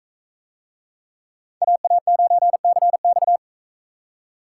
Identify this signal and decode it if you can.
Morse code, heard as a keyed tone, about 44 wpm, 700 Hz. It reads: AA9CX